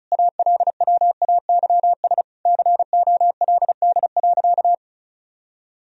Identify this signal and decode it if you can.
Morse code, heard as a keyed tone, about 34 wpm, 700 Hz. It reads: ALWAYS COLD.
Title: Sentences from Top 1000 Words - Review Part 1 - 35wpm